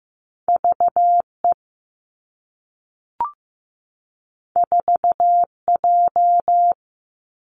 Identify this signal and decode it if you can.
Morse code, heard as a keyed tone, about 15 wpm, 700 Hz. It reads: V E 4 J
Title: Two Challenging Letter-Number Combinations 15wpm